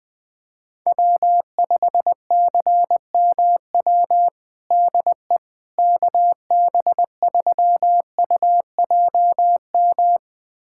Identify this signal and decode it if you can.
Morse code, heard as a keyed tone, about 20 wpm, 700 Hz. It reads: W5CMW DE KB3UJM
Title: USA Callsign DE Callsign 20wpm